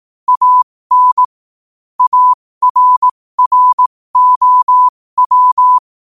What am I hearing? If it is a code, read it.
Morse code, heard as a keyed tone, about 18 wpm, spaced as for 15 wpm, 1000 Hz. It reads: AN ARROW
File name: qotdfiles / 15wpm